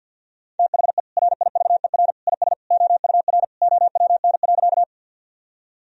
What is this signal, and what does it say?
Morse code, heard as a keyed tone, about 52 wpm, 700 Hz. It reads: THE RIVER IS OUR OWN.
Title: Sentences from Top 1000 Words - Review Part 1 - 50wpm